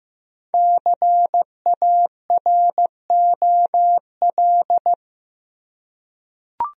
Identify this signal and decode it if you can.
Morse code, heard as a keyed tone, about 16 wpm, 700 Hz. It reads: CAROL
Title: USA Common Names - Max 5 Letters - 15wpm